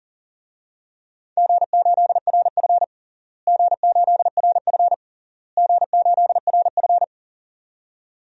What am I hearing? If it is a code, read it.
Morse code, heard as a keyed tone, about 37 wpm, 700 Hz. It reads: G8RF G8RF G8RF
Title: DXCC-100 Call Signs - 4 Characters - Repeated 3x - 40wpm